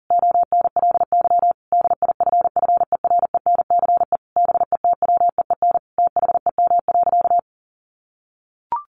Morse code, heard as a keyed tone, about 40 wpm, 700 Hz. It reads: ONLY DIFFERENCE BETWEEN THEM.
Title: Sentences from Top 800 Words 40wpm